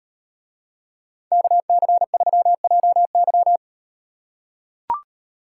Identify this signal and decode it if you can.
Morse code, heard as a keyed tone, about 38 wpm, 700 Hz. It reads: KC3JY